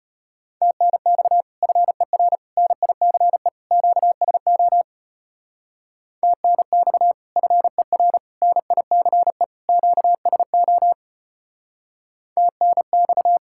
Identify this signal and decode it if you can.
Morse code, heard as a keyed tone, about 30 wpm, 700 Hz. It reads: TNX FER NICE QSO TNX FER NICE QSO TNX
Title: Speed-Racing Common QSO Phrases - 25wpm